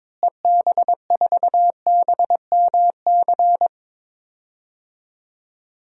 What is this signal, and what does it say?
Morse code, heard as a keyed tone, about 22 wpm, 700 Hz. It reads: EB4BMC
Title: DXCC-100 Call Signs - 6 Characters - Repeated 3x - 22wpm